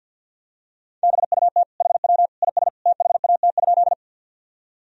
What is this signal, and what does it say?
Morse code, heard as a keyed tone, about 48 wpm, 700 Hz. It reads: BUT HW IS THAT?
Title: Sentences from Top 100 Words and Abbreviations 50wpm